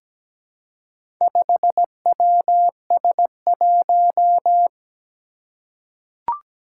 Morse code, heard as a keyed tone, about 17 wpm, 700 Hz. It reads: 5WS1
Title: Four Challenging Letter-Number Combinations 17wpm